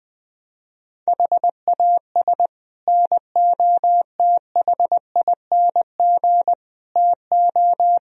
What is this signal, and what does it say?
Morse code, heard as a keyed tone, about 21 wpm, 700 Hz. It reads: HAS NOTHING TO
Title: Speed-Racing - Sets of 5 Words 20wpm